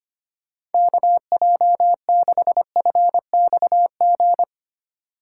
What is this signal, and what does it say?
Morse code, heard as a keyed tone, about 31 wpm, 700 Hz. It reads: KJ6FXG
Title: Speed Racing - USA 2x3 Call Signs 25wpm